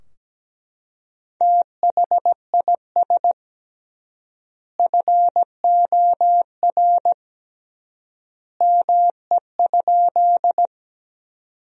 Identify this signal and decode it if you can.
Morse code, heard as a keyed tone, about 17 wpm, 700 Hz. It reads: THIS FOR ME?
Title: 3x Word-Spacing - Sentences from Top 200 Words - 17wpm